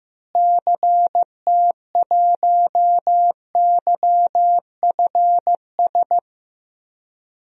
Morse code, heard as a keyed tone, about 15 wpm, 700 Hz. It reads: CT1YFS
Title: DXCC-100 Call Signs - 6 Characters - Repeated 3x - 15wpm